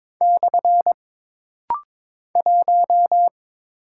Morse code, heard as a keyed tone, about 23 wpm, 700 Hz. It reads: / 1
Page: This rapid-fire timing is geared to push your ICR (Instant Character Recognition) skills.